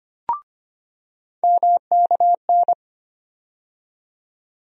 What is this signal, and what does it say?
Morse code, heard as a keyed tone, about 25 wpm, 700 Hz. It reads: MKN
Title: USA Call Sign Postfixes - 3 Letters 25wpm